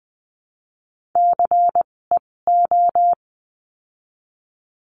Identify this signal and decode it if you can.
Morse code, heard as a keyed tone, about 20 wpm, spaced as for 16 wpm, 700 Hz. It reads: CEO
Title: 256 Lesson 42a - Words 12-20wpm